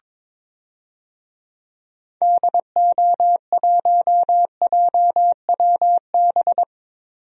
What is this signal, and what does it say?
Morse code, heard as a keyed tone, about 22 wpm, 700 Hz. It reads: DO1JWB
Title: DXCC-100 Call Signs - 6 Characters - 22wpm